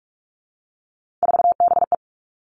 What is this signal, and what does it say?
Morse code, heard as a keyed tone, about 46 wpm, 700 Hz. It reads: 4BE
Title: Three Challenging Letter-Number Combinations 45wpm